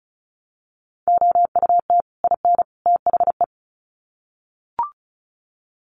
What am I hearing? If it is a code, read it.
Morse code, heard as a keyed tone, about 34 wpm, 700 Hz. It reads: OUT IN THE
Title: Sets of 3 Words 35wpm